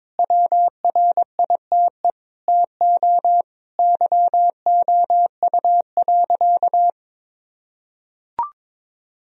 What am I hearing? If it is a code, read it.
Morse code, heard as a keyed tone, about 22 wpm, 700 Hz. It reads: WRITE TO YOU.
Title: Sentences from Top 1000 Words - Review Part 1 - 22wpm